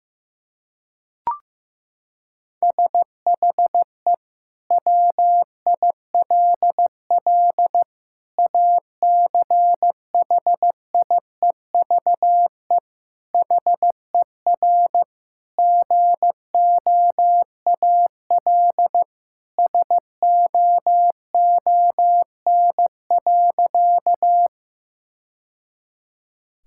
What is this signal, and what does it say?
Morse code, heard as a keyed tone, about 16 wpm, 700 Hz. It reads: SHE WILL ACHIEVE HER GOAL SOON.
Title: Sentences from Top 1000 Words - Review Part 6 - 15wpm